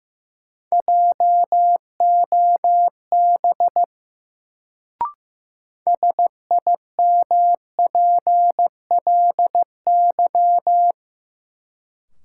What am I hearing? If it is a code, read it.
Morse code, heard as a keyed tone, about 15 wpm, 700 Hz. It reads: JOB SIMPLY
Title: Sentences from Top 600 Words - New Word Drill - 15wpm